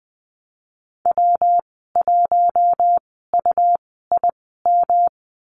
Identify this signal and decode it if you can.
Morse code, heard as a keyed tone, about 20 wpm, spaced as for 15 wpm, 700 Hz. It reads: W1UIM
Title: 147 Lesson 24g - Letter y - Call Signs 20wpm